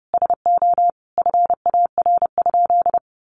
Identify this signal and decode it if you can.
Morse code, heard as a keyed tone, about 30 wpm, 700 Hz. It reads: SO FAR?
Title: Sentences from Top 300 Words 30wpm